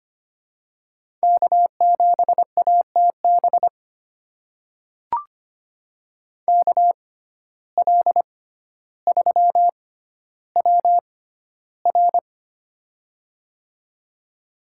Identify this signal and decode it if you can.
Morse code, heard as a keyed tone, about 25 wpm, spaced as for 13 wpm, 700 Hz. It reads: K7ATB KL3WR
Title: USA Call Signs - 5 Characters - 6x Character Spacing - 25wpm